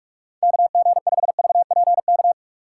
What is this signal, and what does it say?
Morse code, heard as a keyed tone, about 47 wpm, 700 Hz. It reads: KG5VPK
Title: USA Call Signs - 2x Character Spacing - 45wpm